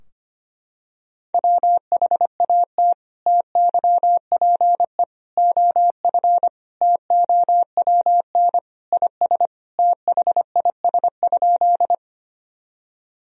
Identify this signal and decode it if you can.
Morse code, heard as a keyed tone, about 25 wpm, 700 Hz. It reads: WHAT TYPE OF TOWN IS THIS?
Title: Sentences from Top 1000 Words - Review Part 3 - 25wpm